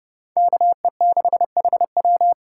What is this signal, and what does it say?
Morse code, heard as a keyed tone, about 29 wpm, 700 Hz. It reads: KE6HW